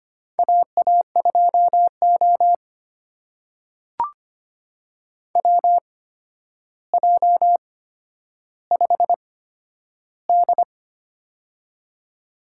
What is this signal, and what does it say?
Morse code, heard as a keyed tone, about 25 wpm, spaced as for 11 wpm, 700 Hz. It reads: AA2O WJ5D
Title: USA Call Signs - 8x Character Spacing - 25wpm